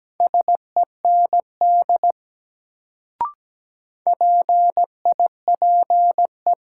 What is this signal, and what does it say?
Morse code, heard as a keyed tone, about 17 wpm, 700 Hz. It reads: SEND PIPE
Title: Sentences from Top 1500 Words - New Word Drill - 17wpm